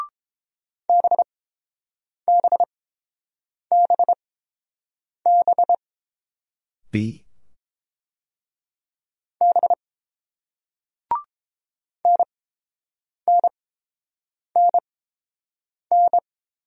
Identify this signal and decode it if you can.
Morse code, heard as a keyed tone, about 30 wpm, spaced as for 7 wpm, 700 Hz. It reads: BBBB B NNNN